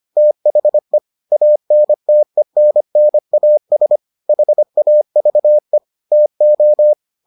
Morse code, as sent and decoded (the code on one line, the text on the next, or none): - .... / .- -. - . -. -. .- ... / .... .- ...- . / - ---
THE ANTENNAS HAVE TO